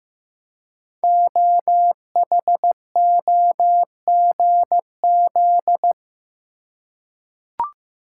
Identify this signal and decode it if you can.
Morse code, heard as a keyed tone, about 15 wpm, 700 Hz. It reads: OHOGZ
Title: Code Groups - 5 Letters - 15wpm